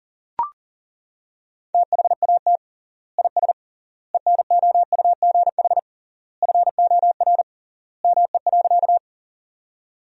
Morse code, heard as a keyed tone, about 40 wpm, 700 Hz. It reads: THAT IS ENOUGH FOR ME.
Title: 3x Word-Spacing - Sentences from Top 400 Words - 40wpm